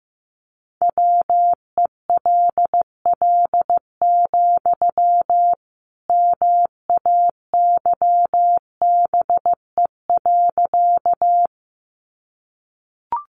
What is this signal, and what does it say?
Morse code, heard as a keyed tone, about 15 wpm, 700 Hz. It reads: WELL, MAYBE.
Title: Sentences from Top 1000 Words 15wpm